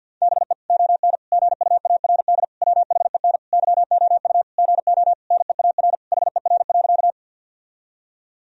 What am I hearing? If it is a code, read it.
Morse code, heard as a keyed tone, about 50 wpm, 700 Hz. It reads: BE ON GUARD WHEN YOU GO NEAR HER.
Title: Sentences from Top 1000 Words - Review Part 4 - 50wpm